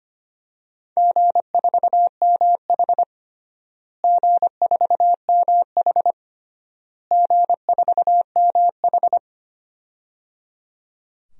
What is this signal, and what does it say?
Morse code, heard as a keyed tone, about 25 wpm, 700 Hz. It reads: G4MH G4MH G4MH
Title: DXCC-100 Call Signs - 4 Characters - Repeated 3x - 25wpm